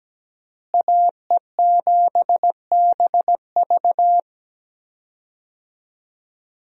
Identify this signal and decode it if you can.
Morse code, heard as a keyed tone, about 17 wpm, 700 Hz. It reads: AE7BV
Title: USA Call Signs - 5 Characters - Repeated 3x - 17wpm